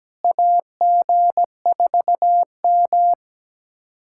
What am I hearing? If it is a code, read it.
Morse code, heard as a keyed tone, about 17 wpm, 700 Hz. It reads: AG4M
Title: USA Callsign DE Callsign - 4 Characters - 17wpm